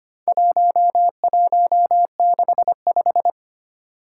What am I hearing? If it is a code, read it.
Morse code, heard as a keyed tone, about 25 wpm, 700 Hz. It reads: 1165
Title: Four Challenging Letter-Number Combinations 25wpm